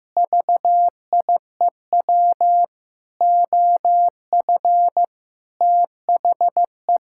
Morse code, heard as a keyed tone, about 16 wpm, 700 Hz. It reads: VIEW OF THE
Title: Sentences from Top 1000 Words - Review Part 6 - 15wpm